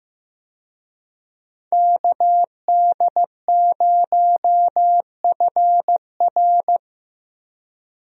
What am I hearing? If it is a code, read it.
Morse code, heard as a keyed tone, about 15 wpm, 700 Hz. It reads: KD0FR
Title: USA Call Signs - 5 Characters - Repeated 3x - 15wpm